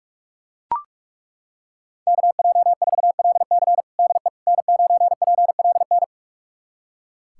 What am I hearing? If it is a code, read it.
Morse code, heard as a keyed tone, about 41 wpm, 700 Hz. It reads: KJ4LC DE N9PLN
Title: USA Callsign DE Callsign - 5 Characters - 45wpm